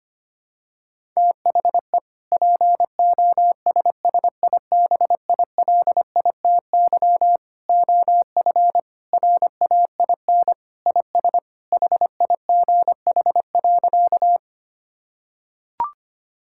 Morse code, heard as a keyed tone, about 25 wpm, 700 Hz. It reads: THE POSSIBILITY OF RAIN IS HIGH.
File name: practice / 25wpm / Sentences with 3-Word Phrases - 25wpm